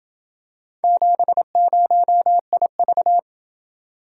Morse code, heard as a keyed tone, about 28 wpm, 700 Hz. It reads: 70IV